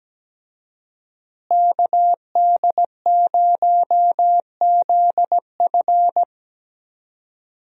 Morse code, heard as a keyed tone, about 17 wpm, 700 Hz. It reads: KD0ZF
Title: USA Call Signs - 5 Characters - Repeated 3x - 17wpm